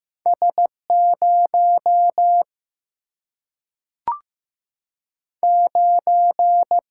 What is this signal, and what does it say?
Morse code, heard as a keyed tone, about 15 wpm, 700 Hz. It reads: S 0 9
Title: Single-Letter Single-Number 15wpm